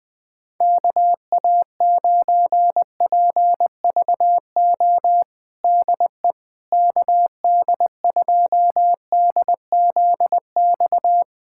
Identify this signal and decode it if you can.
Morse code, heard as a keyed tone, about 20 wpm, 700 Hz. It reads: KA9PVO DE KD2DZX
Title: USA Callsign DE Callsign - 6 Characters - 20wpm